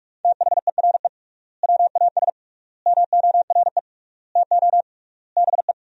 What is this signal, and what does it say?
Morse code, heard as a keyed tone, about 46 wpm, 700 Hz. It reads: THERE WAS MORE TO BE
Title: 3x Word-Spacing - Sentences from Top 500 Words - 45wpm